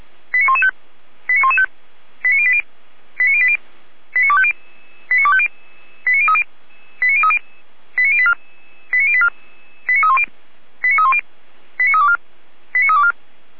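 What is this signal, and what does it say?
Five-tone selective call (ZVEI), heard as quick runs of five tones, tones of about 70 ms. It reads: 89115 89115 89080 89080 89250 89250 89030 89030 89084 89084 89310 89310 89324 89324
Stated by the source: SDR-received ZVEI five-tone call 89270.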